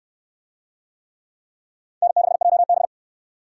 Morse code, heard as a keyed tone, about 69 wpm, 700 Hz. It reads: N6JB